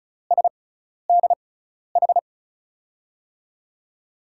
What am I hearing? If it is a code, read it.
Morse code, heard as a keyed tone, about 36 wpm, spaced as for 12 wpm, 700 Hz. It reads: SDH
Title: USA Call Signs - 6 Characters - 6x Character Spacing - 35wpm